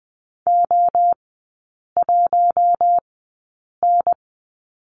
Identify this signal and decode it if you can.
Morse code, heard as a keyed tone, about 20 wpm, spaced as for 8 wpm, 700 Hz. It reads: O1N
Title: 047 Lesson 9e - Number 4 - Three Characters 20wpm